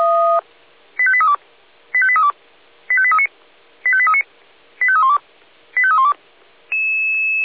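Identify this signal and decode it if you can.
Five-tone selective call (ZVEI), heard as quick runs of five tones, tones of about 70 ms. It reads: 85832 85832 85839 85839 85312 85312
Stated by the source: Received on an SDR; ZVEI five-tone call 85830.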